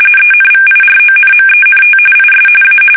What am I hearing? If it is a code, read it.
RTTY, about 45.45 baud. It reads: ENVIRONMENT IS A
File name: fsk850 / teletype-corp